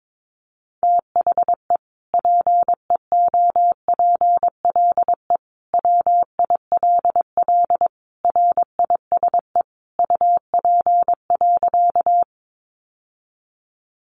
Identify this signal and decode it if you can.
Morse code, heard as a keyed tone, about 22 wpm, 700 Hz. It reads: THE PEOPLE WILL RISE UP.
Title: Sentences from Top 900 Words 22wpm